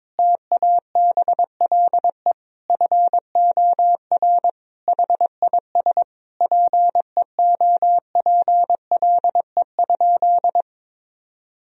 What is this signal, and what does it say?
Morse code, heard as a keyed tone, about 22 wpm, 700 Hz. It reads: TABLE FOR HIS PEOPLE?
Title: Sentences from Top 1000 Words - Review Part 3 - 22wpm